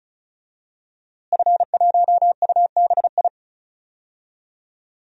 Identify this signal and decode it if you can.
Morse code, heard as a keyed tone, about 36 wpm, 700 Hz. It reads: F1UBI
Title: DXCC-100 Call Signs - 5 Characters - Repeated 3x - 35wpm